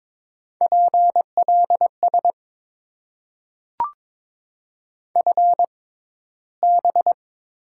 Morse code, heard as a keyed tone, about 22 wpm, spaced as for 14 wpm, 700 Hz. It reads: PLS FB
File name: Top 90 QSO Elements - 6x Character Spacing - 22wpm